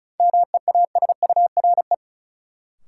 Morse code, heard as a keyed tone, about 34 wpm, 700 Hz. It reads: MEASURE
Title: Sentences from Top 1200 Words - New Word Drill Alt - 35wpm